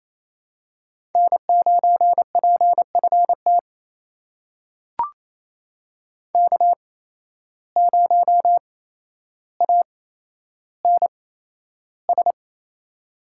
Morse code, heard as a keyed tone, about 28 wpm, spaced as for 12 wpm, 700 Hz. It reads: N9PFT K0ANS
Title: USA Call Signs - 5 Characters - 8x Character Spacing - 28wpm